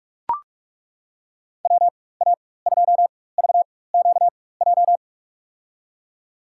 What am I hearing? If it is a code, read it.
Morse code, heard as a keyed tone, about 42 wpm, spaced as for 20 wpm, 700 Hz. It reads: WA2VQJ
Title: USA Call Signs - 6 Characters - 4x Character Spacing - 45wpm